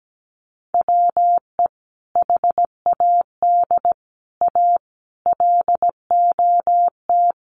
Morse code, heard as a keyed tone, about 17 wpm, 700 Hz. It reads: WE HAD A LOT
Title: Sets of 5 Words Encore 17wpm